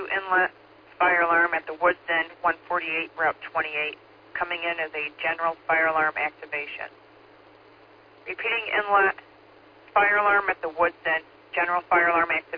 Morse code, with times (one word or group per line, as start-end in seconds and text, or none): none